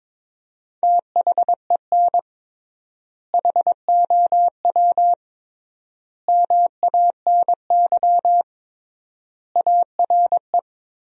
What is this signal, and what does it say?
Morse code, heard as a keyed tone, about 22 wpm, 700 Hz. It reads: THEN HOW MANY ARE
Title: 3x Word-Spacing - Sentences from Top 100 Words - 22wpm